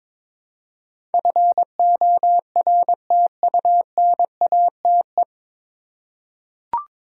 Morse code, heard as a keyed tone, about 22 wpm, 700 Hz. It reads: FORTUNATE
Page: Morse Code Ninja practice 22wpm